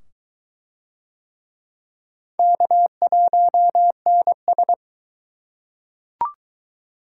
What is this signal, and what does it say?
Morse code, heard as a keyed tone, about 22 wpm, 700 Hz. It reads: K1NS